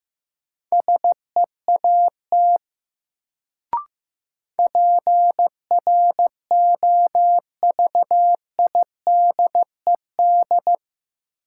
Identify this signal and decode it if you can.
Morse code, heard as a keyed tone, about 15 wpm, 700 Hz. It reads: SEAT PROVIDED
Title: Sentences from Top 1500 Words - New Word Drill - 15wpm